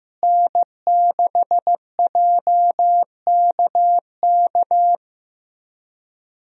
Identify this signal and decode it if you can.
Morse code, heard as a keyed tone, about 15 wpm, 700 Hz. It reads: N6JKK
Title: USA Call Signs - 5 Characters - Repeated 3x - 15wpm